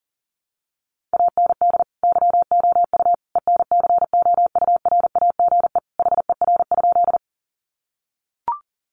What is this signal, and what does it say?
Morse code, heard as a keyed tone, about 37 wpm, 700 Hz. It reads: AND YOU ENCOURAGE HER?
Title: Sentences from Top 1400 Words 40wpm